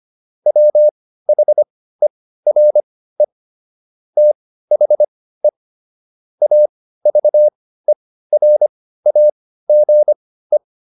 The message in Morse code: WHERE THE AVERAGE

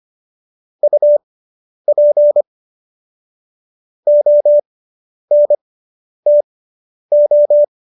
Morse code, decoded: UP ONTO

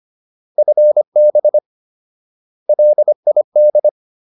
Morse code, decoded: FB LID

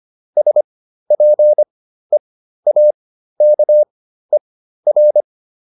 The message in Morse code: SPEAKER

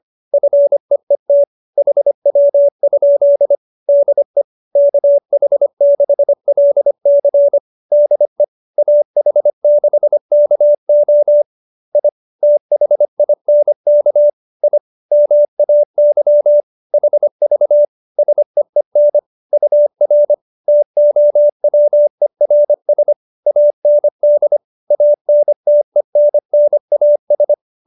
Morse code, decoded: FEET HW? DE KH6LC DE AH6KO I THINK I MAY HV SEEN UR TOWERS AND ANTENNAS

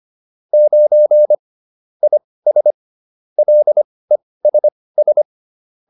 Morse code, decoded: 9 IS LESS

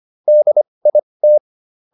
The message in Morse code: DIT

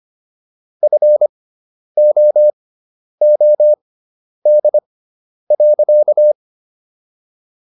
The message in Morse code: FOOD.

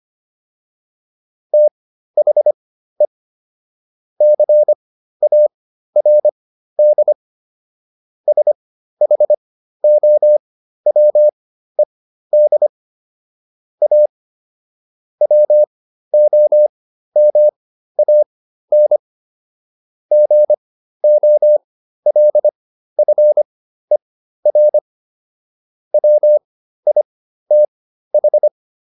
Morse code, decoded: THE CARD SHOWED A WOMAN GOLFER WITH

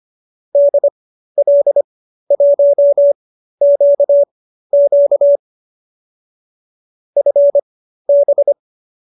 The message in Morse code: DL1QQ FB

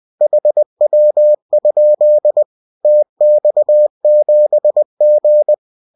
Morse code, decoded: HW? TX7G

